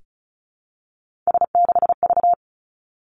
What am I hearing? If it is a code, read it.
Morse code, heard as a keyed tone, about 36 wpm, 700 Hz. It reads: S6V
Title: Three Challenging Letter-Number Combinations 35wpm